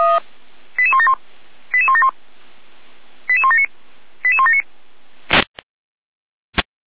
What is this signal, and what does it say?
Five-tone selective call (ZVEI), heard as quick runs of five tones, tones of about 70 ms. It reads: 88171 88171 88179 88179
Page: /455/SDR/ZVEI/88352/